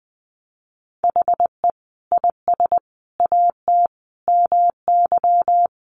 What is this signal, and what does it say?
Morse code, heard as a keyed tone, about 20 wpm, 700 Hz. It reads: HE IS AT MY
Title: Sentences from Top 200 Words 20wpm